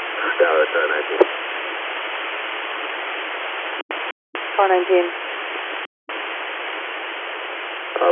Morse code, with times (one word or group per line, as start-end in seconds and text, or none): none